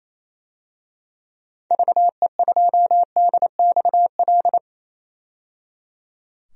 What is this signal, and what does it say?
Morse code, heard as a keyed tone, about 28 wpm, 700 Hz. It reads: VE2DXL
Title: DXCC-100 Call Signs - 6 Characters - 28wpm